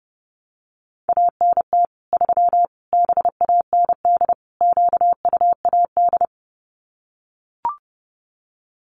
Morse code, heard as a keyed tone, about 30 wpm, 700 Hz. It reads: ANT 3 BAND QUAD
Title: Common QSO Phrases 30wpm